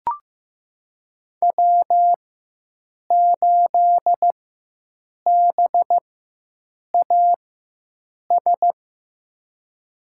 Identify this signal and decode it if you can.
Morse code, heard as a keyed tone, about 16 wpm, spaced as for 7 wpm, 700 Hz. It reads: W8BAS
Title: USA Call Signs - 5 Characters - 4x Character Spacing - 15wpm